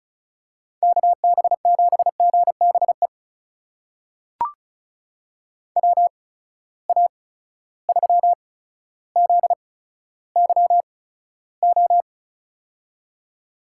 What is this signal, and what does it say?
Morse code, heard as a keyed tone, about 36 wpm, spaced as for 15 wpm, 700 Hz. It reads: KB7GBE WA3ZYO
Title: USA Call Signs - 6 Characters - 8x Character Spacing - 35wpm